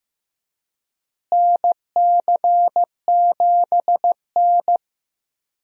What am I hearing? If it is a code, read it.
Morse code, heard as a keyed tone, about 15 wpm, 700 Hz. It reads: NC7N
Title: USA Call Signs - 4 Characters - Repeated 3x - 15wpm